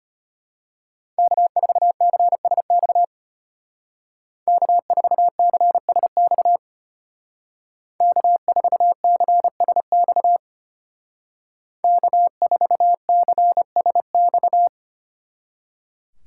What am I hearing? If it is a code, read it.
Morse code, heard as a keyed tone, about 30 wpm, 700 Hz. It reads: K4CSX K4CSX K4CSX K4CSX
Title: Speed Racing - USA 1x3 Call Signs 25wpm